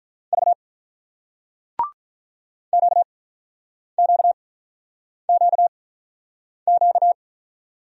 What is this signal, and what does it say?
Morse code, heard as a keyed tone, about 50 wpm, spaced as for 9 wpm, 700 Hz. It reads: V QQQQ